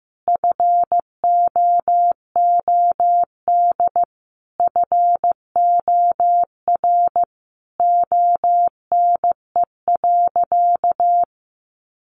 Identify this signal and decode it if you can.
Morse code, heard as a keyed tone, about 15 wpm, 700 Hz. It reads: FOOD FOR ONE.
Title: Sentences from Top 300 Words 15wpm